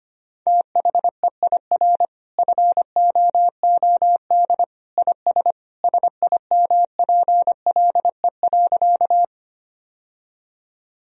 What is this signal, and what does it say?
Morse code, heard as a keyed tone, about 25 wpm, 700 Hz. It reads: THEIR FOOD IS SIMPLE.
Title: Sentences from Top 1000 Words - Review Part 1 - 25wpm